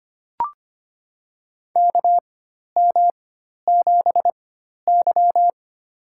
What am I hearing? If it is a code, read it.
Morse code, heard as a keyed tone, about 25 wpm, spaced as for 12 wpm, 700 Hz. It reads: KM7Y
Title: USA Call Signs - 4x Character Spacing - 25wpm